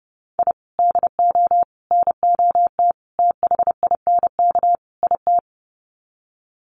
Morse code, heard as a keyed tone, about 30 wpm, 700 Hz. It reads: I DO NOT THINK IT
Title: Sets of 5 Words 30wpm